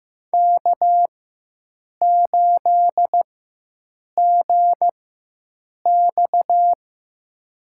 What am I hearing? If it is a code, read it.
Morse code, heard as a keyed tone, about 15 wpm, spaced as for 7 wpm, 700 Hz. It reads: K8GX
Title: USA Call Signs - 4 Characters - 4x Character Spacing - 15wpm